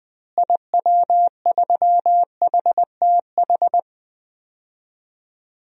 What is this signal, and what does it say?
Morse code, heard as a keyed tone, about 20 wpm, 700 Hz. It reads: IW3HTH
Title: DXCC-100 Call Signs - 6 Characters - Repeated 3x - 20wpm